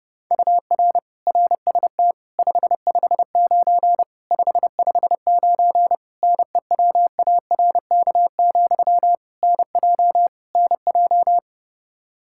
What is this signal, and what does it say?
Morse code, heard as a keyed tone, about 32 wpm, 700 Hz. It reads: UR RST 559 559 NEWARK, NJ NJ
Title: Practice QSO - Exchange 1 - Beginning - 30wpm